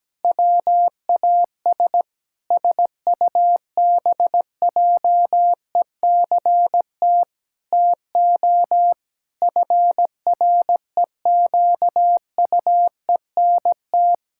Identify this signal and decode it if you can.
Morse code, heard as a keyed tone, about 17 wpm, 700 Hz. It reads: WAS SUBJECT TO FREQUENT
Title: Pangram Challenge v2 - 17wpm